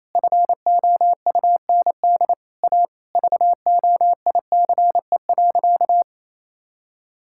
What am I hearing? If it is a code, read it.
Morse code, heard as a keyed tone, about 28 wpm, 700 Hz. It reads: FOUND A VOICE.
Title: Sentences from Top 1600 Words 28wpm